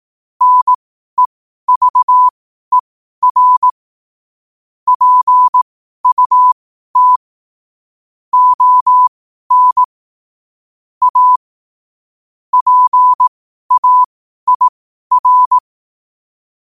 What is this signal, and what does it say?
Morse code, heard as a keyed tone, about 18 wpm, spaced as for 12 wpm, 1000 Hz. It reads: NEVER PUT ON A PAIR
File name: qotdfiles / 12wpm